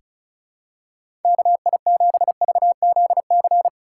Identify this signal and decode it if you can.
Morse code, heard as a keyed tone, about 34 wpm, 690 Hz. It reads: KI7VZC
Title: USA Call Signs - 6 Characters - 3x Character Spacing - 35wpm